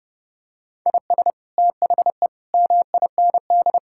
Morse code, heard as a keyed tone, about 30 wpm, 700 Hz. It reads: IS THE MIND
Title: Sentences from Top 1000 Words - Review Part 6 - 30wpm